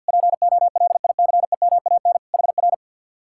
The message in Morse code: POLICEMAN HR

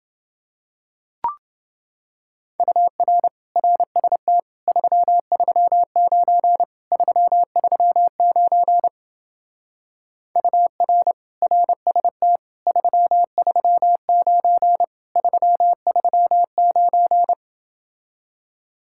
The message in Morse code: UR RST 339 339 UR RST 339 339

